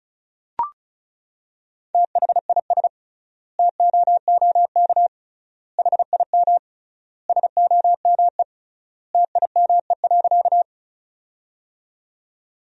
Morse code, heard as a keyed tone, about 34 wpm, 700 Hz. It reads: THIS TOOK HIM SOME TIME.
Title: 3x Word-Spacing - Sentences from Top 300 Words - 35wpm